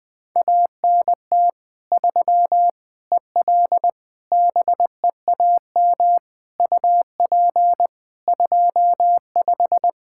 Morse code, heard as a keyed tone, about 20 wpm, 700 Hz. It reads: ANT 3 EL BEAM UP 25
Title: Practice QSO - Exchange 2 - Info - 20wpm